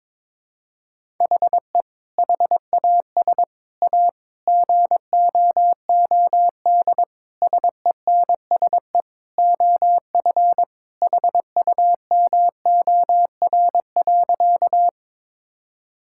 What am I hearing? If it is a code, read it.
Morse code, heard as a keyed tone, about 21 wpm, 700 Hz. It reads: HE HAS A GOOD SENSE OF HUMOR.
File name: practice / 22wpm / Sentences with 2-Word Phrases - 22wpm